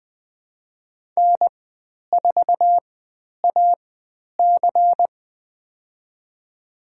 Morse code, heard as a keyed tone, about 20 wpm, spaced as for 10 wpm, 700 Hz. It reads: N4AC
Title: 109 Lesson 18g - Letter m - Call Signs 20wpm